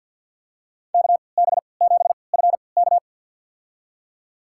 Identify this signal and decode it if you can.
Morse code, heard as a keyed tone, about 52 wpm, spaced as for 29 wpm, 700 Hz. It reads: KB7FK
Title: USA Call Signs - 5 Characters - 3x Character Spacing - 50wpm